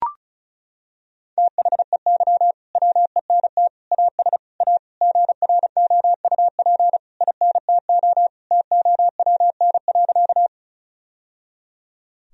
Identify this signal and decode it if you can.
Morse code, heard as a keyed tone, about 34 wpm, 700 Hz. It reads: THEY WENT AS A GROUP INTO TOWN.
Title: Sentences from Top 1000 Words - Review Part 1 - 35wpm